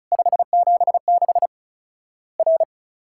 Morse code, heard as a keyed tone, about 36 wpm, 700 Hz. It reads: 576 R
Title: Worked All Europe DX Contest - QTC Exchange - 35wpm